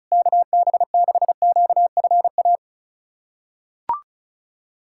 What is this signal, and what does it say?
Morse code, heard as a keyed tone, about 36 wpm, 700 Hz. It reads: KB6QFA
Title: USA Callsign DE Callsign - 6 Characters - 35wpm